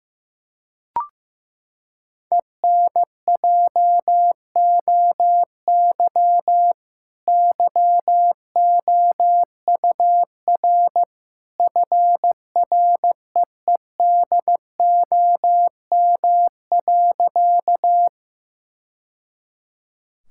Morse code, heard as a keyed tone, about 15 wpm, 700 Hz. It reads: ENJOY YOUR FREEDOM.
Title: Sentences from Top 1000 Words - Review Part 5 - 15wpm